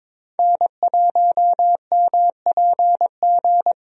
Morse code, heard as a keyed tone, about 21 wpm, 700 Hz. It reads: N1MPG
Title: USA Call Signs - 5 Characters - Repeated 2x - 22wpm